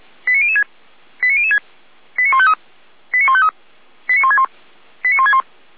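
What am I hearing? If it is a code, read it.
Five-tone selective call (ZVEI), heard as quick runs of five tones, tones of about 70 ms. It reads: 89006 89006 89153 89153 89161 89161
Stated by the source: Five-tone ZVEI selective call received on an SDR.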